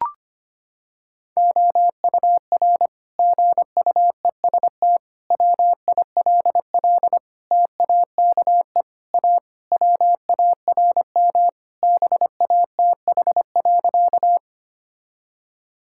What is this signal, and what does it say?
Morse code, heard as a keyed tone, about 25 wpm, 700 Hz. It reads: OUR GUEST WILL TAKE A WARM BATH.
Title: Sentences from Top 1600 Words 25wpm